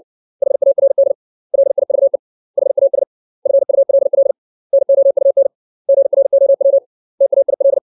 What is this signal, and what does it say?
Morse code, heard as a keyed tone, about 60 wpm, 550 Hz. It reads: HARD LIVE HAS FULL NOUN GROW TAIL
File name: 632 words / 632 Words 60WPM